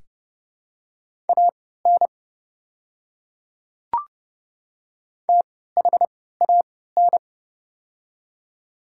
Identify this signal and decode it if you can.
Morse code, heard as a keyed tone, about 31 wpm, spaced as for 17 wpm, 700 Hz. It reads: AN THAN
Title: 3x Character Spacing - Top 200 Words - 30wpm